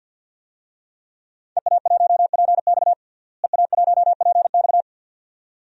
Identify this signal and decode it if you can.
Morse code, heard as a keyed tone, about 52 wpm, 700 Hz. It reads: EA1PX EA1PX
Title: DXCC-100 Call Signs - 5 Characters - Repeated 2x - 50wpm